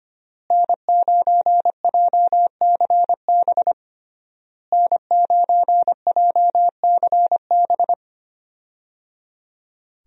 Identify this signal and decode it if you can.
Morse code, heard as a keyed tone, about 25 wpm, 700 Hz. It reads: N9JCB N9JCB
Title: USA Call Signs - 5 Characters - Repeated 3x - 25wpm